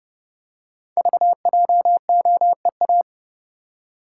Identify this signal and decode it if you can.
Morse code, heard as a keyed tone, about 30 wpm, 700 Hz. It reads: VJOEA
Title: Code Groups - 5 Letters - 30wpm